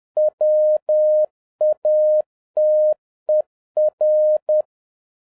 Morse code, heard as a keyed tone, about 10 wpm, 610 Hz. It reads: WATER
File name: 2011-07-mp3 / n36 THE 3.11 2 10wpm